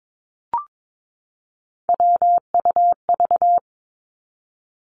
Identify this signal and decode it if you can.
Morse code, heard as a keyed tone, about 22 wpm, 700 Hz. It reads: WUV
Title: Three Challenging Letter-Number Combinations 22wpm